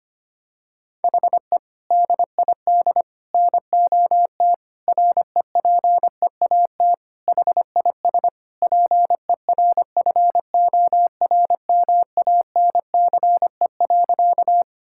Morse code, heard as a keyed tone, about 25 wpm, 700 Hz. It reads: HE DID NOT REPEAT HIS PERFORMANCE.
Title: Sentences from Top 1600 Words 25wpm